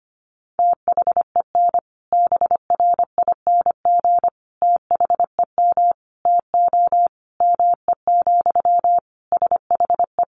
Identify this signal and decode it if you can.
Morse code, heard as a keyed tone, about 25 wpm, 700 Hz. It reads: THEN BRING THEM TO ME, SHE
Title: Sentences from Top 500 Words 25wpm